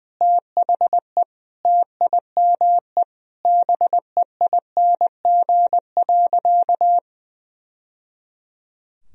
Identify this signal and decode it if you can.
Morse code, heard as a keyed tone, about 20 wpm, 700 Hz. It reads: THE TIME BEING.
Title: Sentences from Top 1000 Words - Review Part 3 - 20wpm